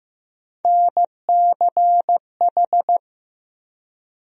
Morse code, heard as a keyed tone, about 16 wpm, 700 Hz. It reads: NCH
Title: USA Call Sign Postfixes - 3 Letters 15wpm